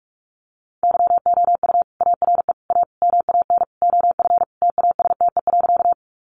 Morse code, heard as a keyed tone, about 44 wpm, 700 Hz. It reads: YOU ARE A MAN OF TASTE.
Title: Sentences from Top 1100 Words 45wpm